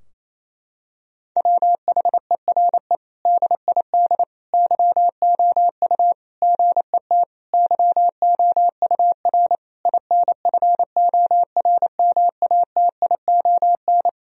Morse code, heard as a keyed tone, about 28 wpm, 700 Hz. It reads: WHERE DID YOU GET YOUR INFORMATION
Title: Sentences from Top 1000 Words - Review Part 3 - 28wpm